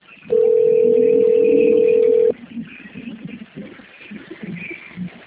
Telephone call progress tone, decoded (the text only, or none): ringback tone